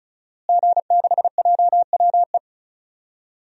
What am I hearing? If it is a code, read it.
Morse code, heard as a keyed tone, about 34 wpm, 700 Hz. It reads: G6JWE
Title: DXCC-100 Call Signs - 5 Characters - Repeated 3x - 35wpm